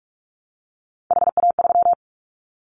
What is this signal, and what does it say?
Morse code, heard as a keyed tone, about 44 wpm, 700 Hz. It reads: HA3
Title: Callsign Prefixes for Top 25 DXCC - 45wpm